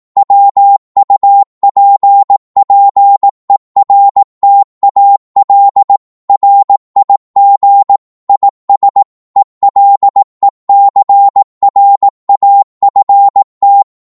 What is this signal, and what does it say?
Morse code, heard as a keyed tone, about 18 wpm, 800 Hz. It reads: WUPPERTAL RIG IS ELECRAFT
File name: Morse / qso-18wpm